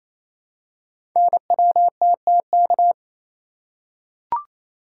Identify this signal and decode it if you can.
Morse code, heard as a keyed tone, about 28 wpm, 700 Hz. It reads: NWTTK
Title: Code Groups - 5 Letters - 28wpm